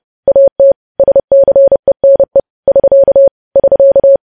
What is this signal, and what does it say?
Morse code, heard as a keyed tone, about 30 wpm, 550 Hz. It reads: AT SCENE <SK> <SK>
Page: news headlines in morse code at 30 wpm